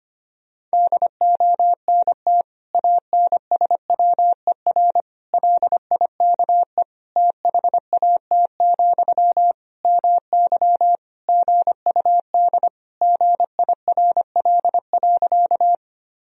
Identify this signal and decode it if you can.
Morse code, heard as a keyed tone, about 25 wpm, 700 Hz. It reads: DONT ANSWER LIKE THAT, MY GUD GIRL.
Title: Sentences from Top 400 Words and Abbreviations 25wpm